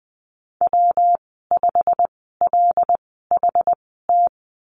Morse code, heard as a keyed tone, about 20 wpm, spaced as for 15 wpm, 700 Hz. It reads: W5LHT
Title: 193 Lesson 31g - Letter v - Call Signs 20wpm